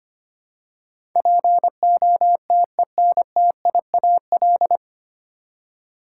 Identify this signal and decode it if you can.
Morse code, heard as a keyed tone, about 25 wpm, 700 Hz. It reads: POTENTIAL
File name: Nine Letter Words - 25wpm